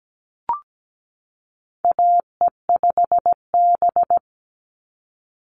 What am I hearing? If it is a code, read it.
Morse code, heard as a keyed tone, about 17 wpm, 700 Hz. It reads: AE5B
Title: NA SOTA Chaser Callsigns 17wpm